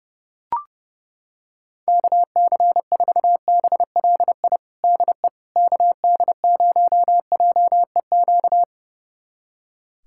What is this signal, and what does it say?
Morse code, heard as a keyed tone, about 32 wpm, 700 Hz. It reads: KC4BLI DE KD0JEQ
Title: USA Callsign DE Callsign 30wpm